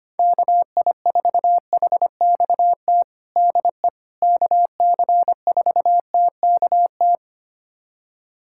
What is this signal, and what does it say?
Morse code, heard as a keyed tone, about 25 wpm, 700 Hz. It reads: KI4HXT DE KC4TKT
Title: USA Callsign DE Callsign - 6 Characters - 25wpm